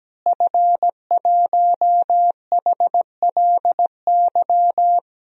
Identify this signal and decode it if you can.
Morse code, heard as a keyed tone, about 17 wpm, 700 Hz. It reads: F1HLY
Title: DXCC-100 Call Signs - 5 Characters - Repeated 3x - 17wpm